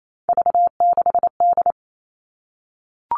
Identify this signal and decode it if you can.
Morse code, heard as a keyed tone, about 28 wpm, 700 Hz. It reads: V6D
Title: All Cut Numbers - 3-Digit 28wpm